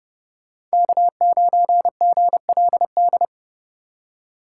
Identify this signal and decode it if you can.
Morse code, heard as a keyed tone, about 33 wpm, 700 Hz. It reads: K9GLD